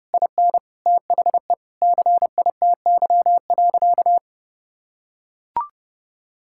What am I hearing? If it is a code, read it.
Morse code, heard as a keyed tone, about 30 wpm, 700 Hz. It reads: IN THE CITY.
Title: Sentences from Top 1000 Words - Review Part 6 - 30wpm